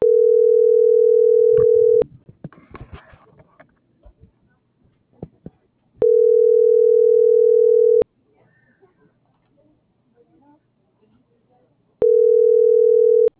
Ringback tone (North America), 440 + 480 Hz.